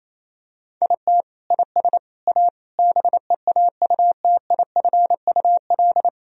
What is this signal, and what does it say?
Morse code, heard as a keyed tone, about 28 wpm, 700 Hz. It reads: IT IS A BEAUTIFUL
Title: Sentences from Top 1000 Words - Review Part 2 - 28wpm